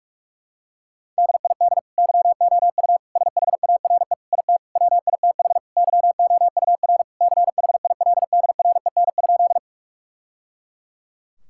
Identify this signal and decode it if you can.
Morse code, heard as a keyed tone, about 44 wpm, 700 Hz. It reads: DID YOU SHARE IT WITH YOUR CHILDREN?
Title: Sentences from Top 1000 Words - Review Part 6 - 45wpm